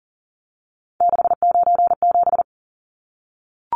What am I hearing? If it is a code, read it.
Morse code, heard as a keyed tone, about 40 wpm, 700 Hz. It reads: 697
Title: Numbers - 3-Digit 40wpm